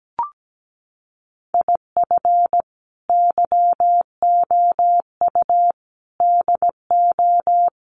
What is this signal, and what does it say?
Morse code, heard as a keyed tone, about 17 wpm, 700 Hz. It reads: IF YOU DO
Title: Sets of 4 Words 17wpm